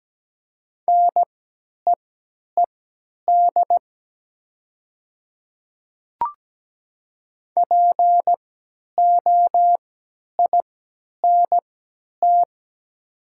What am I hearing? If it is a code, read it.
Morse code, heard as a keyed tone, about 17 wpm, spaced as for 10 wpm, 700 Hz. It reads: NEED POINT
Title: 3x Character Spacing - Top 200 Words - 17wpm